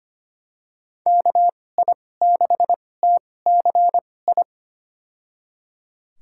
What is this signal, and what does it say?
Morse code, heard as a keyed tone, about 25 wpm, spaced as for 18 wpm, 690 Hz. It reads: KI6TCI
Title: USA Call Signs - 6 Characters - 2x Character Spacing - 25wpm